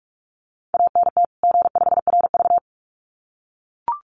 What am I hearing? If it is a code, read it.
Morse code, heard as a keyed tone, about 44 wpm, 700 Hz. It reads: ANT G5RV
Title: Common QSO Phrases 45wpm